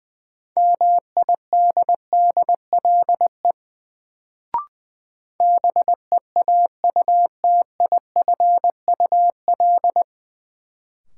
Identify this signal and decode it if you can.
Morse code, heard as a keyed tone, about 21 wpm, 700 Hz. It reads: MIDDLE BEAUTIFUL
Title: Sentences from Top 900 Words - New Word Drill - 20wpm